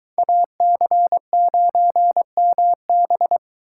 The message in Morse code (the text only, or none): AC9MB